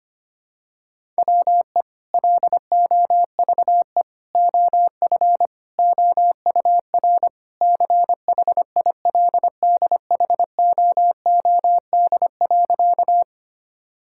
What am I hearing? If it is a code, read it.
Morse code, heard as a keyed tone, about 25 wpm, 700 Hz. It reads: WE LOVE OF OUR CHILDHOOD.
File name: practice / 25wpm / Sentences with 2-Word Phrases - 25wpm